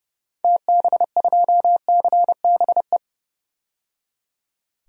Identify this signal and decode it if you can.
Morse code, heard as a keyed tone, about 31 wpm, 700 Hz. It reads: TB2CBE
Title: DXCC-100 Call Signs - 6 Characters - 30wpm